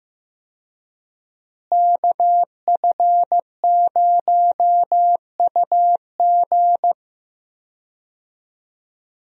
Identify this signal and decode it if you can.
Morse code, heard as a keyed tone, about 15 wpm, 700 Hz. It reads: KF0UG
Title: USA Call Signs - 5 Characters - Repeated 3x - 15wpm